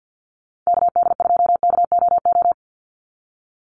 Morse code, heard as a keyed tone, about 50 wpm, 700 Hz. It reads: KD2KOO
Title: USA Call Signs 50wpm